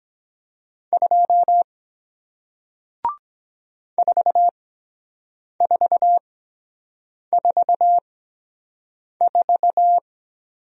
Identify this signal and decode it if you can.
Morse code, heard as a keyed tone, about 23 wpm, spaced as for 7 wpm, 700 Hz. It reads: 2 4444